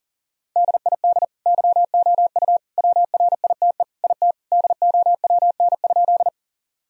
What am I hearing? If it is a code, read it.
Morse code, heard as a keyed tone, about 40 wpm, 700 Hz. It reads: DID YOU WRITE IT DOWN?
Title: Sentences from Top 1000 Words - Review Part 1 - 40wpm